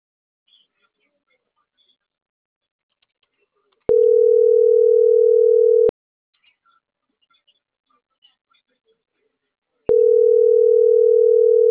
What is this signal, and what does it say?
Ringback tone (North America), 440 + 480 Hz.